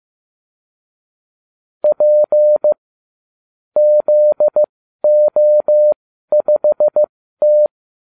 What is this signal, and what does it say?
Morse code, heard as a keyed tone, about 15 wpm, spaced as for 12 wpm, 600 Hz. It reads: P ZO5T